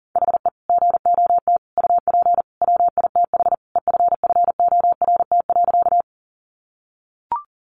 Morse code, heard as a keyed tone, about 40 wpm, 700 Hz. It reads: HE GOT UP WITH EFFORT.
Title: Sentences from Top 1200 Words 40wpm